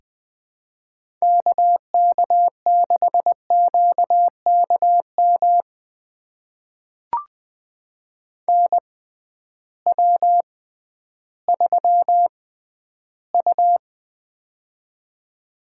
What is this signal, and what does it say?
Morse code, heard as a keyed tone, about 20 wpm, spaced as for 12 wpm, 700 Hz. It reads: KK6QKM NW3U
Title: USA Call Signs - 6x Character Spacing - 20wpm